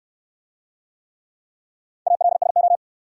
Morse code, heard as a keyed tone, about 68 wpm, 700 Hz. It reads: ALSO